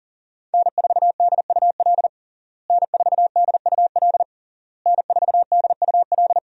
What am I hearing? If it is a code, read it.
Morse code, heard as a keyed tone, about 38 wpm, 700 Hz. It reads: N4DUL N4DUL N4DUL
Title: USA Call Signs - 5 Characters - Repeated 3x - 40wpm